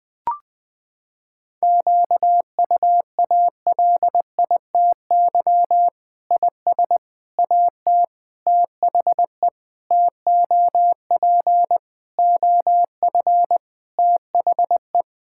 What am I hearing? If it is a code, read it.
Morse code, heard as a keyed tone, about 19 wpm, 700 Hz. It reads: QUALITY IS AT THE TOP OF THE
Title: Sentences with 6-Word Phrases - 20wpm